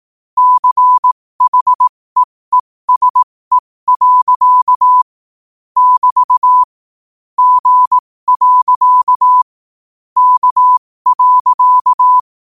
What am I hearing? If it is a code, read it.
Morse code, heard as a keyed tone, about 18 wpm, spaced as for 15 wpm, 1000 Hz. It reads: CHEESE. <BT> G. K.
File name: qotdfiles / 15wpm